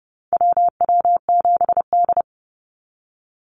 Morse code, heard as a keyed tone, about 30 wpm, 700 Hz. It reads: WW7D
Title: NA SOTA Activator Callsigns 30wpm